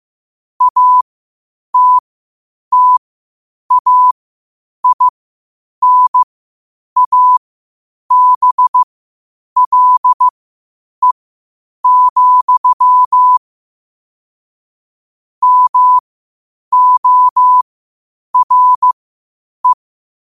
Morse code, heard as a keyed tone, about 15 wpm, spaced as for 8 wpm, 1000 Hz. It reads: ATTAINABLE, MORE
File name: qotdfiles / 8wpm